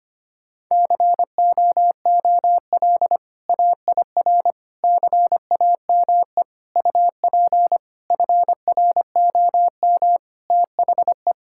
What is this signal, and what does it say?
Morse code, heard as a keyed tone, about 25 wpm, 700 Hz. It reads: COOL AIR CAME UP FROM THE
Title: Sentences from Top 1600 Words 25wpm